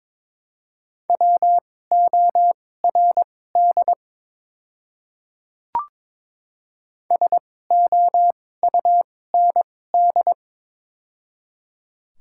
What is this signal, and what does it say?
Morse code, heard as a keyed tone, about 21 wpm, spaced as for 16 wpm, 700 Hz. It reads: WORD SOUND
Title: 2x Character Spacing - Top 300 Words - 22wpm